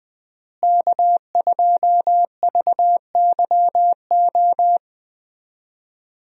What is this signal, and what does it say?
Morse code, heard as a keyed tone, about 21 wpm, 700 Hz. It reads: K2VYO
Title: USA Call Signs - 5 Characters - Repeated 3x - 20wpm